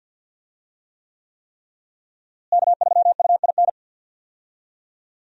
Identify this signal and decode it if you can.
Morse code, heard as a keyed tone, about 47 wpm, 700 Hz. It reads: K3UIN
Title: USA Call Signs - 5 Characters - Repeated 3x - 50wpm